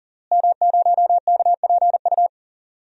Morse code, heard as a keyed tone, about 40 wpm, 700 Hz. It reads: M0KPU
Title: DXCC-100 Call Signs - 5 Characters - Repeated 3x - 40wpm